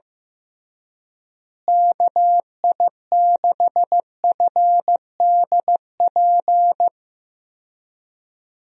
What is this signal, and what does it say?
Morse code, heard as a keyed tone, about 15 wpm, 700 Hz. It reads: KI6FDP
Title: USA Call Signs - 6 Characters - Repeated 3x - 15wpm